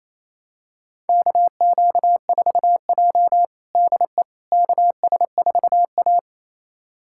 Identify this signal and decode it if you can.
Morse code, heard as a keyed tone, about 28 wpm, 700 Hz. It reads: KQ4J DE KS4A
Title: USA Callsign DE Callsign - 4 Characters - 28wpm